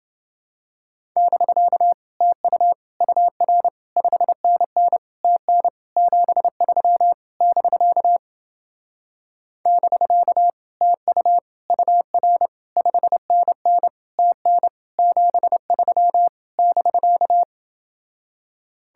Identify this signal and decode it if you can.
Morse code, heard as a keyed tone, about 28 wpm, 700 Hz. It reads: <BK> TU UR 5NN TN 73 <BK> <BK> TU UR 5NN TN 73 <BK>